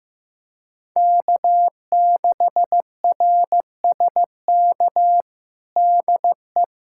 Morse code, heard as a keyed tone, about 15 wpm, 700 Hz. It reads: K6RSK DE
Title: USA Callsign DE Callsign - 5 Characters - 15wpm